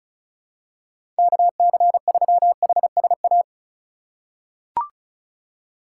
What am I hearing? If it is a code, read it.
Morse code, heard as a keyed tone, about 36 wpm, 700 Hz. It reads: KC3HSA
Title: USA Call Signs - 6 Characters - 4x Character Spacing - 35wpm